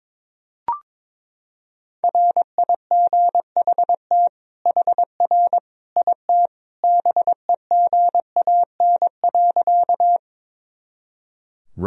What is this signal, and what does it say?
Morse code, heard as a keyed tone, about 21 wpm, 700 Hz. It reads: RIGHT HR IT BEGAN.
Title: Sentences from Top 300 Words and Abbreviations 22wpm